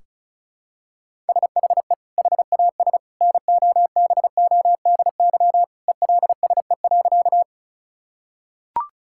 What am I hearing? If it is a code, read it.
Morse code, heard as a keyed tone, about 34 wpm, 700 Hz. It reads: SHE HAS NOBODY ELSE.
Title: Sentences from Top 1600 Words 35wpm